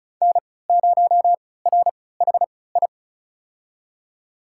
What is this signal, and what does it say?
Morse code, heard as a keyed tone, about 36 wpm, spaced as for 20 wpm, 700 Hz. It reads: N0RHI
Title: USA Call Signs - 3x Character Spacing - 35wpm